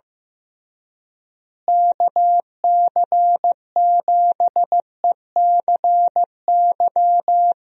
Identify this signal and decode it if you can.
Morse code, heard as a keyed tone, about 15 wpm, 700 Hz. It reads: KC7ECY
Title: USA Call Signs - 6 Characters - Repeated 3x - 15wpm